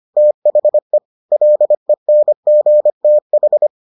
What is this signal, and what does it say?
Morse code, heard as a keyed tone, about 25 wpm, 600 Hz. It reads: THE LENGTH